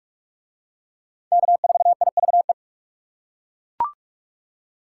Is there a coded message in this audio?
Yes: Morse code from 1.3 to 2.5 s, heard as a keyed tone, about 41 wpm, 700 Hz. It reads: K4IVE